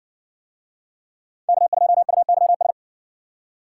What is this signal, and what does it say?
Morse code, heard as a keyed tone, about 60 wpm, 700 Hz. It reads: K2UYS